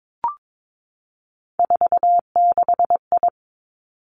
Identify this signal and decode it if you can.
Morse code, heard as a keyed tone, about 22 wpm, 700 Hz. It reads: 46I
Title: Three Challenging Letter-Number Combinations 22wpm